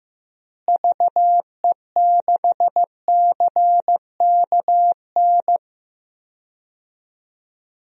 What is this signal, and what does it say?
Morse code, heard as a keyed tone, about 14 wpm, 700 Hz. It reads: VE6CKN
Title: DXCC-100 Call Signs - 6 Characters - Repeated 3x - 15wpm